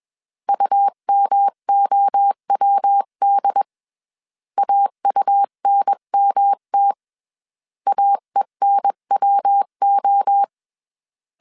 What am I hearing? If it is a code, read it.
Morse code, heard as a keyed tone, about 21 wpm, 800 Hz. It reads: UMOWD AUNMT AENWO
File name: CW / DET / 1 3 1500C S B 20wpm 1754